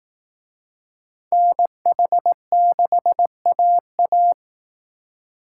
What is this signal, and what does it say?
Morse code, heard as a keyed tone, about 18 wpm, 700 Hz. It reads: NH6AA
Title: Speed Racing - USA 2x2 Call Signs 15wpm